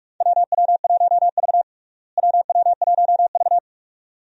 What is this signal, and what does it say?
Morse code, heard as a keyed tone, about 46 wpm, 700 Hz. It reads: WW1V WW1V
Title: USA Call Signs - 4 Characters - Repeated 2x - 45wpm